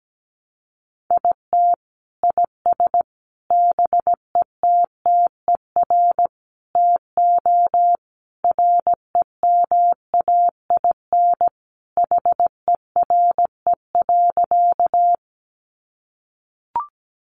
Morse code, heard as a keyed tone, about 17 wpm, 700 Hz. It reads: IT IS BETTER TO REMAIN HERE.
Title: Sentences from Top 1200 Words 17wpm